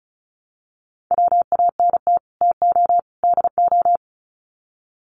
Morse code, heard as a keyed tone, about 34 wpm, 700 Hz. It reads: WANT TO DO
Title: Sets of 3 Words 35wpm